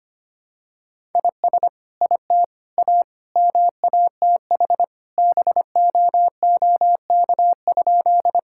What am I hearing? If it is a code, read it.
Morse code, heard as a keyed tone, about 25 wpm, 700 Hz. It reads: IS IT A MATH BOOK?
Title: Sentences from Top 1000 Words - Review Part 6 - 25wpm